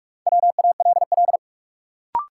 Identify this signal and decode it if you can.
Morse code, heard as a keyed tone, about 42 wpm, 700 Hz. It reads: WALL
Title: Sentences from Top 1000 Words - Review Part 6 - New Word Drill - 45wpm